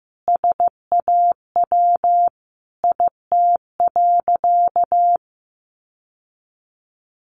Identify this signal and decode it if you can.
Morse code, heard as a keyed tone, about 15 wpm, 700 Hz. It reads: SAW IT.
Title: Sentences from Top 1500 Words 15wpm